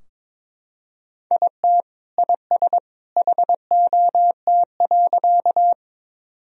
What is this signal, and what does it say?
Morse code, heard as a keyed tone, about 22 wpm, 700 Hz. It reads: IT IS HOT.
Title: Sentences from Top 1000 Words - Review Part 3 - 22wpm